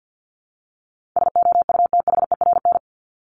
Morse code, heard as a keyed tone, about 50 wpm, 700 Hz. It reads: SOUTHERN
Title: Top 1751-2000 Words 50wpm